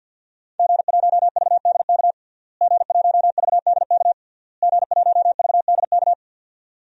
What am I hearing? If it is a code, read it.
Morse code, heard as a keyed tone, about 51 wpm, 700 Hz. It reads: G1VDK G1VDK G1VDK
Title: DXCC-100 Call Signs - 5 Characters - Repeated 3x - 50wpm